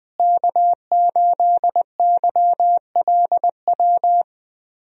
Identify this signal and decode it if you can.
Morse code, heard as a keyed tone, about 20 wpm, 700 Hz. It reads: K8YLW
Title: USA Call Signs - 5 Characters - Repeated 3x - 20wpm